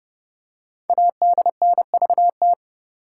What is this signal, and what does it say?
Morse code, heard as a keyed tone, about 30 wpm, 700 Hz. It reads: ADNVT
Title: Code Groups - 5 Letters - 30wpm